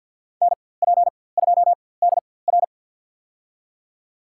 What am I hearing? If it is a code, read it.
Morse code, heard as a keyed tone, about 53 wpm, spaced as for 24 wpm, 700 Hz. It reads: NP2DR
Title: USA Call Signs - 5 Characters - 4x Character Spacing - 50wpm